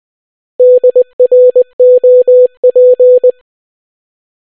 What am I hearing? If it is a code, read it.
Morse code, heard as a keyed tone, about 20 wpm, 500 Hz. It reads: DROP